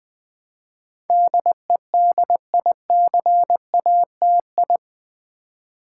Morse code, heard as a keyed tone, about 20 wpm, 700 Hz. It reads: DEDICATI
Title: Build A Word - 4 Letters At A Time - 10-Letter Words - 20wpm